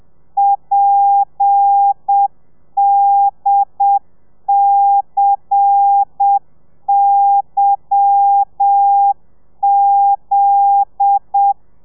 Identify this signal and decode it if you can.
Morse code, heard as a keyed tone, about 7 wpm, 800 Hz. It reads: PDCYZ
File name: telegrafia / i2tfj / Lez10 30